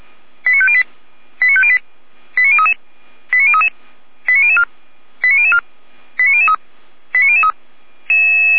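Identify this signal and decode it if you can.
Five-tone selective call (ZVEI), heard as quick runs of five tones, tones of about 70 ms. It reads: 89508 89508 89033 89033 89004 89004 89003 89003
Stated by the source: ZVEI selective call received by SDR, fire brigade audio archive of feuerwehr-orlamuende.org.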